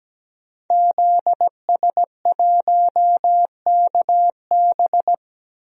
Morse code, heard as a keyed tone, about 17 wpm, 700 Hz. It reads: ZS1KB
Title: DXCC-100 Call Signs - 5 Characters - Repeated 2x - 17wpm